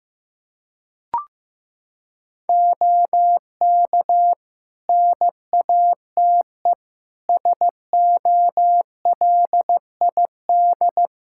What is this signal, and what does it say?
Morse code, heard as a keyed tone, about 15 wpm, 700 Hz. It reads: OK NATE SOLID
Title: Practice QSO - Exchange 3 - More Info - 15wpm